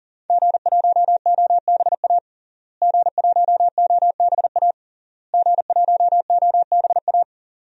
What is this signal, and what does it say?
Morse code, heard as a keyed tone, about 40 wpm, 700 Hz. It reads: G1OBA G1OBA G1OBA
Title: DXCC-100 Call Signs - 5 Characters - Repeated 3x - 40wpm